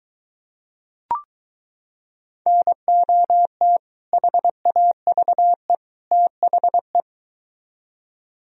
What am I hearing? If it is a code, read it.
Morse code, heard as a keyed tone, about 23 wpm, 700 Hz. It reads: NOT HAVE THE